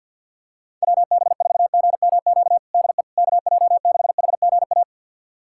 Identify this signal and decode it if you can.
Morse code, heard as a keyed tone, about 51 wpm, 700 Hz. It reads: WB4GMQ DE KJ6HGA
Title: USA Callsign DE Callsign - 6 Characters - 50wpm